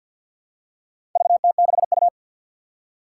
Morse code, heard as a keyed tone, about 53 wpm, 700 Hz. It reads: VT6U
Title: Four Challenging Letter-Number Combinations 50wpm